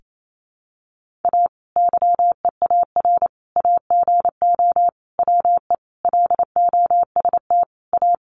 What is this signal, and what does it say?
Morse code, heard as a keyed tone, about 28 wpm, 700 Hz. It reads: A YEAR AGO WE LOST A
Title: Sentences from Top 1400 Words 28wpm